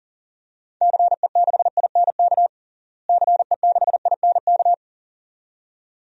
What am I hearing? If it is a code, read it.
Morse code, heard as a keyed tone, about 40 wpm, 700 Hz. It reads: CE6INK CE6INK
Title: DXCC-100 Call Signs - 6 Characters - Repeated 2x - 40wpm